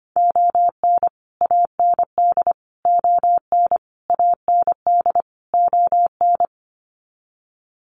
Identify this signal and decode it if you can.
Morse code, heard as a keyed tone, about 25 wpm, 700 Hz. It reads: ON AND ON AND ON
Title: Sets of 5 Words Encore2 25wpm